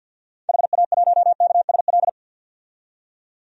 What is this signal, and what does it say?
Morse code, heard as a keyed tone, about 53 wpm, 700 Hz. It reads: HA1KSL
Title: DXCC-100 Call Signs - 6 Characters - 50wpm